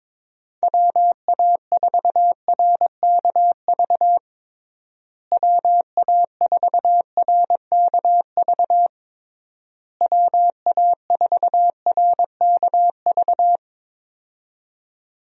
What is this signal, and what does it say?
Morse code, heard as a keyed tone, about 22 wpm, 700 Hz. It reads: WA4RKV WA4RKV WA4RKV
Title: USA Call Signs - 6 Characters - Repeated 3x - 22wpm